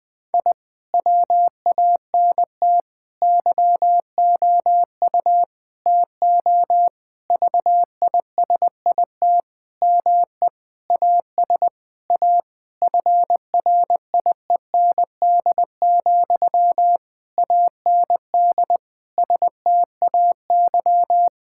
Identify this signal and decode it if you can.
Morse code, heard as a keyed tone, about 20 wpm, 700 Hz. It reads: I WANT YOU TO VISIT ME AS A FRIEND, AND STAY
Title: Sentences from Top 1000 Words - Review Part 6 - 20wpm